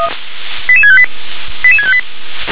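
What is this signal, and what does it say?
Five-tone selective call (ZVEI), heard as quick runs of five tones, tones of about 70 ms. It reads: 88658 88658
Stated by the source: Received on an SDR; ZVEI five-tone selective call.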